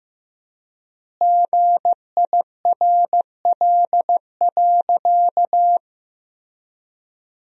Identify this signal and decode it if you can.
Morse code, heard as a keyed tone, about 15 wpm, 700 Hz. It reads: GIRL.
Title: 3x Word-Spacing - Sentences from Top 400 Words - 15wpm